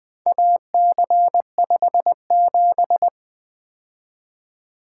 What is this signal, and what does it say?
Morse code, heard as a keyed tone, about 24 wpm, 700 Hz. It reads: AC57
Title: Speed-Racing - Four Letter-Number Combinations - 20wpm